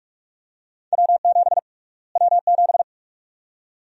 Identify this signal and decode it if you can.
Morse code, heard as a keyed tone, about 47 wpm, 700 Hz. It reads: W7 W7
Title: USA Call Sign Prefixes - Repeated 2x - 45wpm